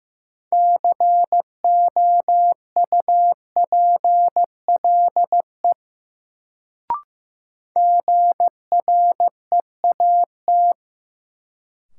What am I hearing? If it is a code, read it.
Morse code, heard as a keyed tone, about 15 wpm, 700 Hz. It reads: COUPLE GREAT
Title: Sets of 2 Words - New Word Drill - 15wpm